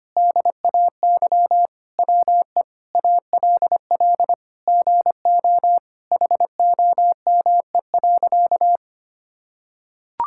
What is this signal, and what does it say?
Morse code, heard as a keyed tone, about 25 wpm, 700 Hz. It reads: DAY WE ALL GO HOME.
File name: practice / 25wpm / Sentences with 4-Word Phrases - 25wpm